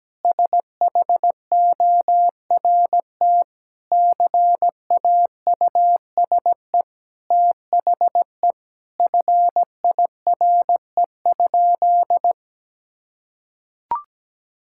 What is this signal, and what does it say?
Morse code, heard as a keyed tone, about 17 wpm, 700 Hz. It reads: SHORT CAUSE THE FIRE?
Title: Sentences from Top 1000 Words - Review Part 6 - 17wpm